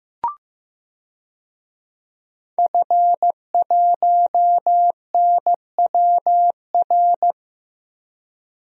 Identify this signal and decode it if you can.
Morse code, heard as a keyed tone, about 15 wpm, 700 Hz. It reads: F1NWR
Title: DXCC-100 Call Signs - 5 Characters - Repeated 2x - 15wpm